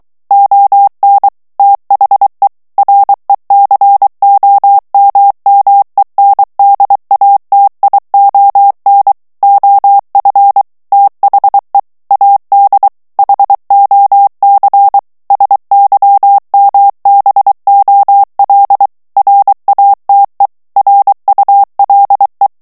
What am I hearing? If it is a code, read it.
Morse code, heard as a keyed tone, about 23 wpm, 800 Hz. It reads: ON THE RECOMMENDATION OF THE AD HOC SYMBOL RATE RULE
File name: ANGLAIS texte clair 3 vitesse 22 mots minute